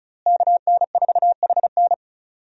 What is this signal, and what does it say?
Morse code, heard as a keyed tone, about 34 wpm, 700 Hz. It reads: KN4HN
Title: USA Call Signs - 5 Characters - Repeated 3x - 35wpm